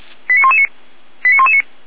Five-tone selective call (ZVEI), heard as quick runs of five tones, tones of about 70 ms. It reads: 89119 89119